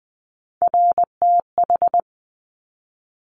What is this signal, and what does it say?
Morse code, heard as a keyed tone, about 20 wpm, 700 Hz. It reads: RTH
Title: Top Three Letter Combinations in Words 20wpm